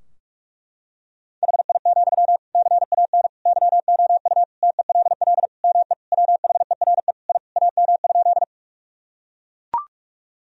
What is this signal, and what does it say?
Morse code, heard as a keyed tone, about 41 wpm, 700 Hz. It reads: HI, CAN YOU TELL ME WHERE I AM?
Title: Sentences from Top 1000 Words - Review Part 3 - 45wpm